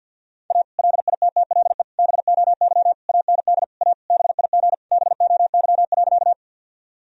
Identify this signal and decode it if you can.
Morse code, heard as a keyed tone, about 51 wpm, 700 Hz. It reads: A LITTLE BOY AND A BIG BOY.
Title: Sentences from Top 1000 Words - Review Part 1 - 50wpm